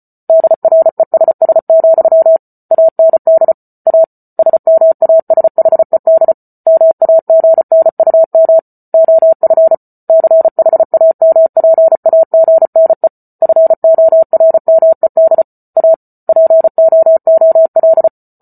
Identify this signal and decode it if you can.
Morse code, heard as a keyed tone, about 34 wpm, 650 Hz. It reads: DRESS, AND A SMASHED MAGNUM OF CHAMPAGNE FORMED A POOL